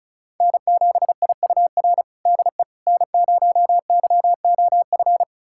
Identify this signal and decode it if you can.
Morse code, heard as a keyed tone, about 34 wpm, 690 Hz. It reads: N7IUR DE N0YOF
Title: USA Callsign DE Callsign - 5 Characters - 35wpm